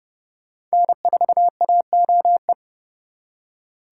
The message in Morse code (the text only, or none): N4AOE